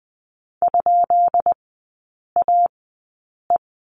Morse code, heard as a keyed tone, about 20 wpm, spaced as for 8 wpm, 690 Hz. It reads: ?AE
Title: 133 Lesson 22e - Question Mark - Three Characters 20wpm